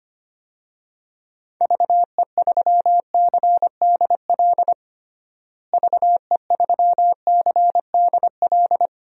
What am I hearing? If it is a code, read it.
Morse code, heard as a keyed tone, about 25 wpm, 700 Hz. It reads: VE3CDL VE3CDL
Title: DXCC-100 Call Signs - 6 Characters - Repeated 2x - 25wpm